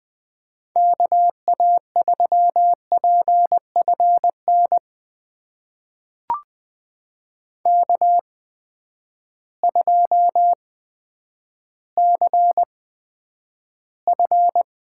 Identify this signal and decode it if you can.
Morse code, heard as a keyed tone, about 20 wpm, spaced as for 10 wpm, 700 Hz. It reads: KA3PFN K2CF
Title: USA Call Signs - 8x Character Spacing - 20wpm